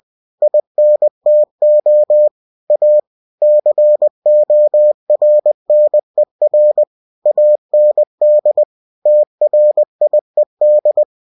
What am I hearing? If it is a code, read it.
Morse code, heard as a keyed tone, about 20 wpm, 600 Hz. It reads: INTO A CORNER AND TRIED